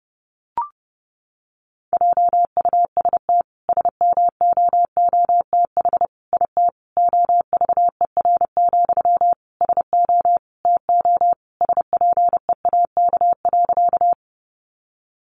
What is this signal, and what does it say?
Morse code, heard as a keyed tone, about 30 wpm, 700 Hz. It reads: JUST SMOOTH IT OVER, SO TO SPEAK.
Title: Sentences from Top 1400 Words 30wpm